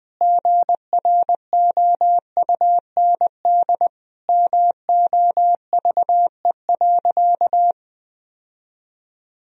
Morse code, heard as a keyed tone, about 20 wpm, 700 Hz. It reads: GROUND MOVE.
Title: Sentences from Top 1000 Words - Review Part 2 - 20wpm